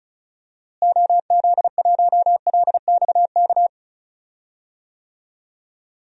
Morse code, heard as a keyed tone, about 36 wpm, 700 Hz. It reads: OZ1LXK
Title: DXCC-100 Call Signs - 6 Characters - Repeated 2x - 35wpm